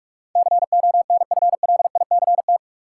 Morse code, heard as a keyed tone, about 44 wpm, 700 Hz. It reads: CONFLICT
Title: Sentences from Top 1300 Words - New Word Drill Alt - 45wpm